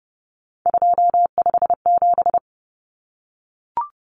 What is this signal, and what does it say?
Morse code, heard as a keyed tone, about 30 wpm, 700 Hz. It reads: 257
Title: Common Cut Numbers - 3-Digit 30wpm